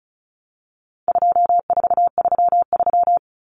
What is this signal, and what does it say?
Morse code, heard as a keyed tone, about 34 wpm, 700 Hz. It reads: 2433
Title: Common Cut Numbers - 4-Digit 35wpm